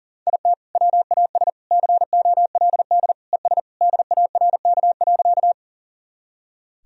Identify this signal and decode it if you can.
Morse code, heard as a keyed tone, about 37 wpm, 700 Hz. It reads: IT WAS COLD ES DARK.
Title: Sentences from Top 500 Words and Abbreviations 40wpm